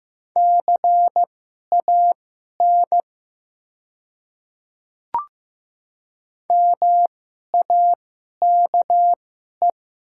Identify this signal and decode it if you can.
Morse code, heard as a keyed tone, about 15 wpm, spaced as for 11 wpm, 700 Hz. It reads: CAN MAKE
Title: 2x Character Spacing - Top 100 Words - 15wpm